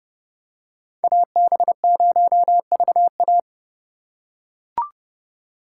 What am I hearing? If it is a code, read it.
Morse code, heard as a keyed tone, about 29 wpm, 700 Hz. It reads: AB0VA